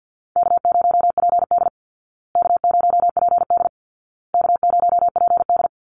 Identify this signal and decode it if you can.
Morse code, heard as a keyed tone, about 50 wpm, 700 Hz. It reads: K0PD K0PD K0PD
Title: Easy Call Signs Repeated 3x - 50wpm